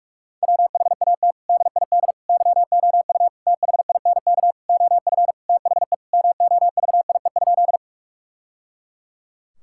Morse code, heard as a keyed tone, about 46 wpm, 700 Hz. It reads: WHAT DID YOU THINK OF THE MOVIE?
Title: Sentences from Top 1000 Words - Review Part 5 - 45wpm